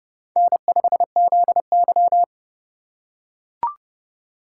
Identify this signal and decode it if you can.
Morse code, heard as a keyed tone, about 30 wpm, 700 Hz. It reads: N5ZY
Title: USA Call Signs - 4 Characters - 4x Character Spacing - 30wpm